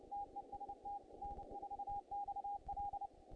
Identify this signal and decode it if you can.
Morse code, heard as a keyed tone, about 29 wpm, 810 Hz. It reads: TEST N4XL
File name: contests / naqp-cw-k5tr-op-k5pi- -2300